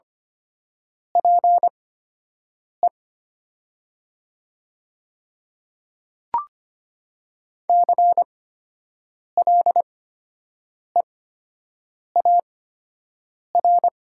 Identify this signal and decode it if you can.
Morse code, heard as a keyed tone, about 25 wpm, spaced as for 7 wpm, 700 Hz. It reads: PE CLEAR
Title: 8x Character Spacing - Top 500 Words - 25wpm